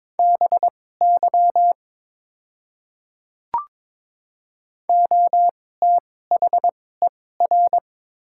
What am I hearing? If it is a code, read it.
Morse code, heard as a keyed tone, about 22 wpm, spaced as for 16 wpm, 700 Hz. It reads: BY OTHER
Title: 2x Character Spacing - Top 300 Words - 22wpm